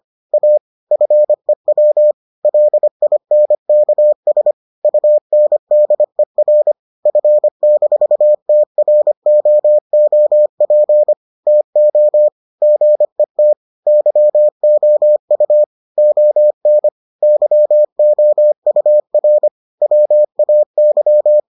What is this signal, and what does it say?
Morse code, heard as a keyed tone, about 25 wpm, 600 Hz. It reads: A FEW LINKS UNDER F-TROOP TO GET YOU ON YOUR WAY